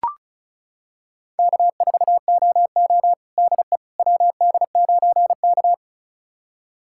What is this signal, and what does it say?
Morse code, heard as a keyed tone, about 36 wpm, 700 Hz. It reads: K4OO DE WD9K
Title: USA Callsign DE Callsign - 4 Characters - 35wpm